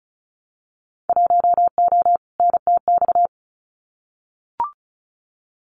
Morse code, heard as a keyed tone, about 33 wpm, 700 Hz. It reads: 1O NTX
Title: Winter Field Day Exchanges Encore 35wpm